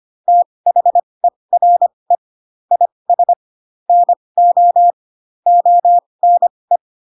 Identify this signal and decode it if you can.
Morse code, heard as a keyed tone, about 25 wpm, spaced as for 20 wpm, 700 Hz. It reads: THERE IS NO ONE
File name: Pinocchio / Pinocchio ch1 15 25 ik1hge